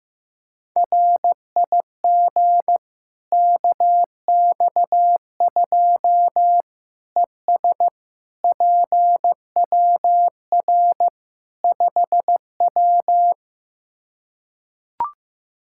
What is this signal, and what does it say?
Morse code, heard as a keyed tone, about 15 wpm, 700 Hz. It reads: RIG KX2 ES PWR 5W
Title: K1USN SST QSO Practice 15wpm